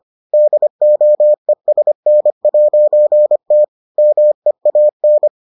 Morse code, heard as a keyed tone, about 25 wpm, 600 Hz. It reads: DOESN'T MEAN